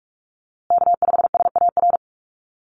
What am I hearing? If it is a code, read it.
Morse code, heard as a keyed tone, about 44 wpm, 700 Hz. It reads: K5SAR
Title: USA Call Signs Repeated 3x 45wpm